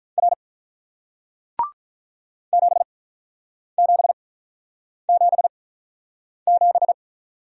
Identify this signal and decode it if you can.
Morse code, heard as a keyed tone, about 48 wpm, spaced as for 9 wpm, 700 Hz. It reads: R 7777